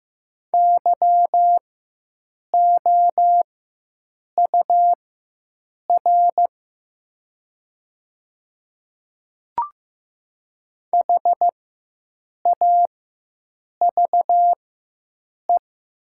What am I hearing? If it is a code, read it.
Morse code, heard as a keyed tone, about 15 wpm, spaced as for 7 wpm, 700 Hz. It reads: YOUR HAVE
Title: 4x Character Spacing - Top 100 Words - 15wpm